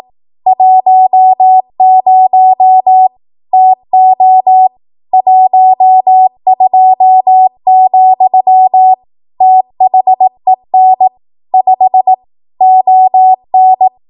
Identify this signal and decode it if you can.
Morse code, heard as a keyed tone, about 18 wpm, 750 Hz. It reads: 10 TO 12, THEN 5 ON
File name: Archive / 18 WPM / 18WPM